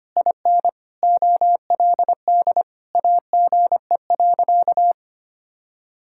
Morse code, heard as a keyed tone, about 25 wpm, 700 Hz. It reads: IN OLD AGE.
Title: Sentences from Top 1000 Words - Review Part 6 - 25wpm